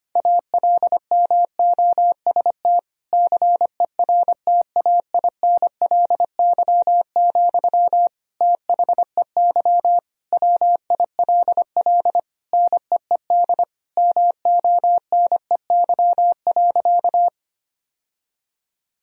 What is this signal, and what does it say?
Morse code, heard as a keyed tone, about 25 wpm, 700 Hz. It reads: ALMOST CERTAINLY, THEY WILL NEED MONEY.
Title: Sentences from Top 1000 Words - Review Part 4 - 25wpm